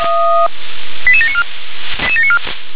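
Five-tone selective call (ZVEI), heard as quick runs of five tones, tones of about 70 ms. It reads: 88794
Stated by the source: ZVEI five-tone selective call 88718 received with an SDR.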